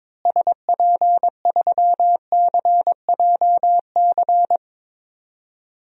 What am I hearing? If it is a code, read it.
Morse code, heard as a keyed tone, about 22 wpm, 700 Hz. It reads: SP3CJC
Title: DXCC-100 Call Signs - 6 Characters - Repeated 3x - 22wpm